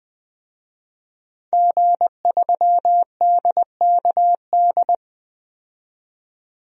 Morse code, heard as a keyed tone, about 20 wpm, 700 Hz. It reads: G3DKD
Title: DXCC-100 Call Signs - 5 Characters - 20wpm